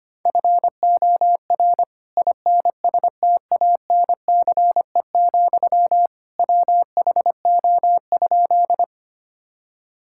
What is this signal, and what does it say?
Morse code, heard as a keyed tone, about 25 wpm, 700 Hz. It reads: FOR INSTANCE, WHO?
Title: Sentences from Top 1000 Words - Review Part 5 - 25wpm